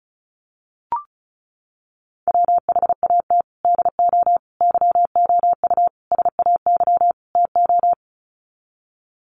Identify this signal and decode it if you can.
Morse code, heard as a keyed tone, about 34 wpm, 700 Hz. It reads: WHAT DO YOU SAY TO
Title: Sets of 5 Words 35wpm